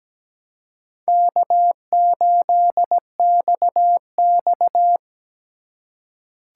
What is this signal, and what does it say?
Morse code, heard as a keyed tone, about 17 wpm, 700 Hz. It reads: K8XX
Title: USA Call Signs - 2x Character Spacing - 17wpm